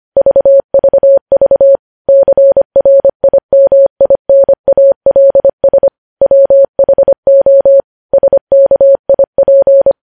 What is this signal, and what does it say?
Morse code, heard as a keyed tone, about 25 wpm, 550 Hz. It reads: VVV CRIMINALS WHO SKIP
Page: news headlines in morse code at 25 wpm